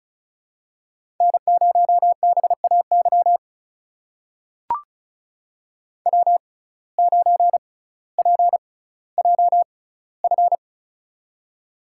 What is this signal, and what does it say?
Morse code, heard as a keyed tone, about 36 wpm, spaced as for 18 wpm, 700 Hz. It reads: N0BAY W9PJF
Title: USA Call Signs - 5 Characters - 6x Character Spacing - 35wpm